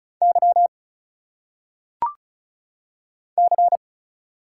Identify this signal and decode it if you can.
Morse code, heard as a keyed tone, about 35 wpm, 700 Hz. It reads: Y C
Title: Single Letters - Phonetic Pronunciation 35wpm